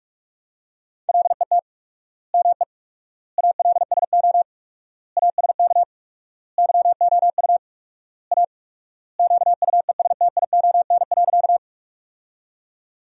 Morse code, heard as a keyed tone, about 46 wpm, 700 Hz. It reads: LET ME ALSO ASK YOU A QUESTION.
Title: 4x Word-Spacing - Sentences from Top 400 Words - 45wpm